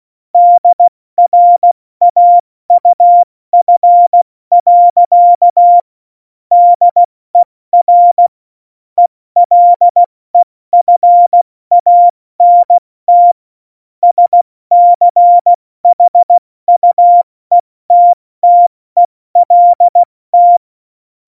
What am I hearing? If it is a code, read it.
Morse code, heard as a keyed tone, about 16 wpm, 700 Hz. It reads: DRAUF. DER ELEFANT SCHUETTELT
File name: Morsekurs / Witze / Witz 03.Juli 201, 70 F80